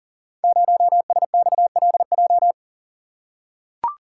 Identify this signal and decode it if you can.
Morse code, heard as a keyed tone, about 39 wpm, 700 Hz. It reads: 0SXLJ
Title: Code Groups - 5 Letter:Numbers - 40wpm